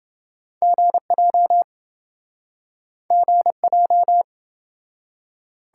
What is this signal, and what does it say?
Morse code, heard as a keyed tone, about 30 wpm, 700 Hz. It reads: GJ GJ